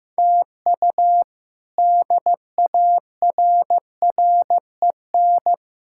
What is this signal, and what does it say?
Morse code, heard as a keyed tone, about 15 wpm, 700 Hz. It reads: TU DARREN
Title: K1USN SST QSO Practice - 15wpm